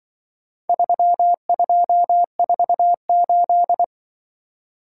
Morse code, heard as a keyed tone, about 23 wpm, 700 Hz. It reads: 3248
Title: Speed-Racing - Four Letter-Number Combinations - 20wpm